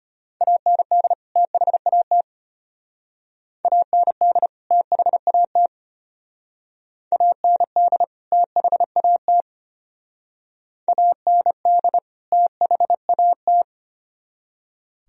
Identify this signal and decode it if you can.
Morse code, heard as a keyed tone, about 32 wpm, 700 Hz. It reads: AND THAT AND THAT AND THAT AND THAT